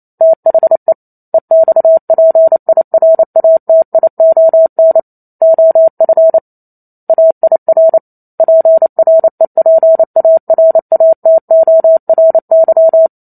Morse code, heard as a keyed tone, about 28 wpm, 650 Hz. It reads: THE EXPIRATION OF AIR PREPARATORY
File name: Chapter 2 - 28 WPM